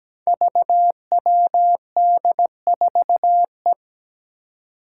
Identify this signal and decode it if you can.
Morse code, heard as a keyed tone, about 17 wpm, 700 Hz. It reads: VWD4E
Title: Five Challenging Letter-Number Combinations 17wpm